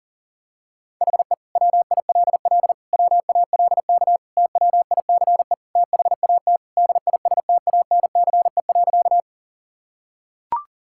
Morse code, heard as a keyed tone, about 40 wpm, 700 Hz. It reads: HE WILL WALK TWICE THAT DISTANCE.
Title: Sentences from Top 1600 Words 40wpm